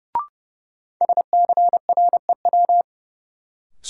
Morse code, heard as a keyed tone, about 30 wpm, 700 Hz. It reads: SCREW
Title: Sentences from Top 1600 Words - New Word Drill Alt - 30wpm